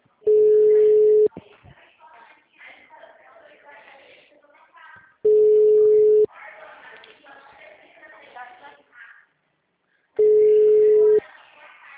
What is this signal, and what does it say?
Ringback tone (Europe, 425 Hz), with speech.
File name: out- -2009- - - .532